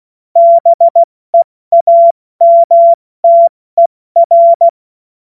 Morse code, heard as a keyed tone, about 16 wpm, 680 Hz. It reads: BEAMTER
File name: Witz-23-September-2025-70-F80